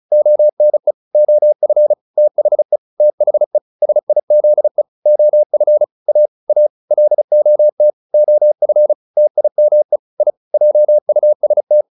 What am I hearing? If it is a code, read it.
Morse code, heard as a keyed tone, about 34 wpm, 600 Hz. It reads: ONE OF THE THE SIZE OF A A LOT OF TIME I JUST